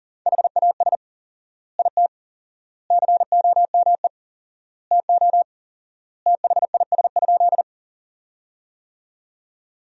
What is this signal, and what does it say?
Morse code, heard as a keyed tone, about 40 wpm, 700 Hz. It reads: HAS IT COME TO THIS?
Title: 4x Word-Spacing - Sentences from Top 100 Words - 40wpm